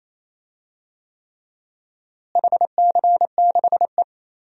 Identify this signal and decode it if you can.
Morse code, heard as a keyed tone, about 28 wpm, 700 Hz. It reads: HC6E
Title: DXCC-100 Call Signs - 4 Characters - Repeated 3x - 28wpm